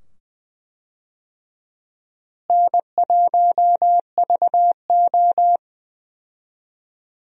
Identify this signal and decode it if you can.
Morse code, heard as a keyed tone, about 20 wpm, 700 Hz. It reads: N1VO
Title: USA Call Signs - 4 Characters - Repeated 2x - 20wpm